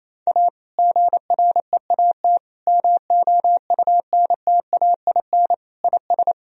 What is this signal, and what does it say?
Morse code, heard as a keyed tone, about 28 wpm, 700 Hz. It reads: A GREAT MOUNTAIN IS
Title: Sentences from Top 1000 Words - Review Part 1 - 28wpm